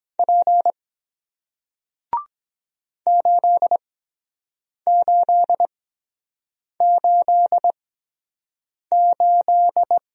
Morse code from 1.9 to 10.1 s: 8888

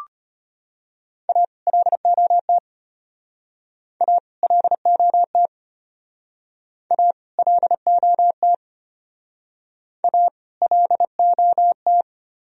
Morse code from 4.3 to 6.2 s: LOT

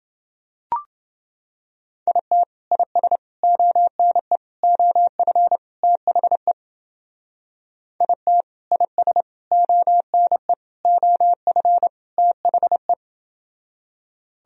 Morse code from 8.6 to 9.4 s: IS

